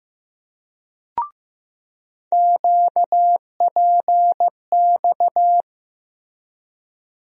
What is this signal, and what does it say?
Morse code, heard as a keyed tone, about 15 wpm, 700 Hz. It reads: QPX
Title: USA Call Sign Postfixes - 3 Letters 15wpm